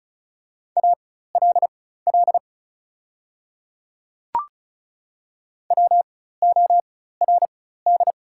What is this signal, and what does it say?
Morse code, heard as a keyed tone, about 36 wpm, spaced as for 16 wpm, 690 Hz. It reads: ALL WORD
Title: 4x Character Spacing - Top 300 Words - 35wpm